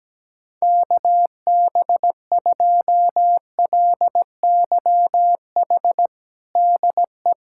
Morse code, heard as a keyed tone, about 17 wpm, 700 Hz. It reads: KB2LYH DE
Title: USA Callsign DE Callsign 17wpm